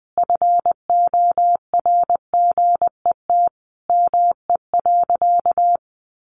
Morse code, heard as a keyed tone, about 20 wpm, 700 Hz. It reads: FORGET ME.
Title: Sentences from Top 1200 Words 20wpm